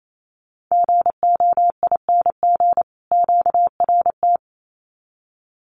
Morse code, heard as a keyed tone, about 28 wpm, 700 Hz. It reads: GOING QRT
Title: Common QSO Phrases 28wpm